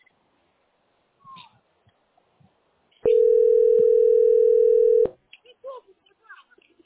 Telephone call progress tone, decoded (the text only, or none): ringback tone